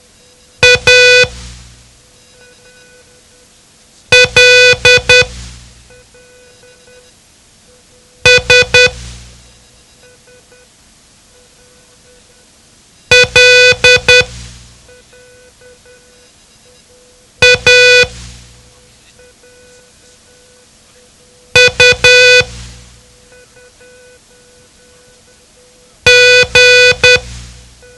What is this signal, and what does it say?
Morse code, heard as a keyed tone, about 10 wpm, spaced as for 2 wpm, 500 Hz. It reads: ALS LAUG
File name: ham radio / CodeQuick / Code Quick 07